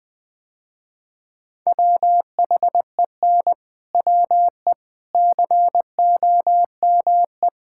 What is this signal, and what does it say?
Morse code, heard as a keyed tone, about 20 wpm, 700 Hz. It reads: WHEN WE COME
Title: Sentences with 4-Word Phrases - Spoken Beginning - 20wpm